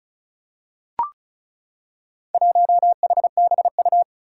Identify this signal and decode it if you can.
Morse code, heard as a keyed tone, about 35 wpm, 700 Hz. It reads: 1HBU
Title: Four Challenging Letter-Number Combinations 35wpm